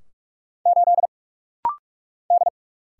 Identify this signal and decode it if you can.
Morse code, heard as a keyed tone, about 46 wpm, 700 Hz. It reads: 8 D
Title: Single Letter-Number - Rapid-Fire 45wpm